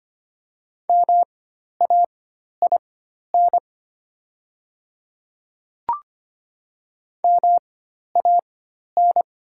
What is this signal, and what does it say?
Morse code, heard as a keyed tone, about 25 wpm, spaced as for 12 wpm, 700 Hz. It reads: MAIN MAN
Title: 4x Character Spacing - Top 300 Words - 25wpm